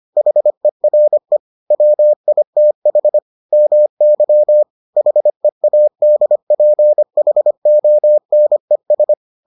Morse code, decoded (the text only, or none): HERE WITH MY HEADPHONES